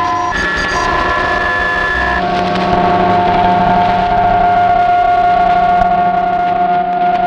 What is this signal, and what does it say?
PSK31, about 31.25 baud.
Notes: The radio was tuned to the 19 meters band